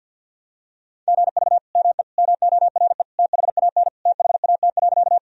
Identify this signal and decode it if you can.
Morse code, heard as a keyed tone, about 52 wpm, 700 Hz. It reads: GV ME MORE THAN THAT.
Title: Sentences from Top 200 Words and Abbreviations 50wpm